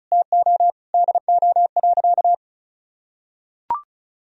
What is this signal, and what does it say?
Morse code, heard as a keyed tone, about 36 wpm, 700 Hz. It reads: TO DO.
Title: Sentences from Top 1000 Words - Review Part 2 - 35wpm